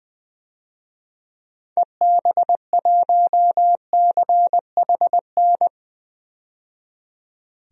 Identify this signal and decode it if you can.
Morse code, heard as a keyed tone, about 20 wpm, 700 Hz. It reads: EB1CHN
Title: DXCC-100 Call Signs - 6 Characters - 20wpm